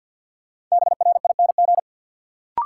Morse code, heard as a keyed tone, about 46 wpm, 700 Hz. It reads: BRING
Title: Sentences from Top 500 Words - New Word Drill - 50wpm